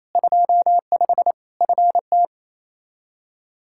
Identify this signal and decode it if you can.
Morse code, heard as a keyed tone, about 28 wpm, 700 Hz. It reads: 25 FT
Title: K1USN SST QSO Practice 28wpm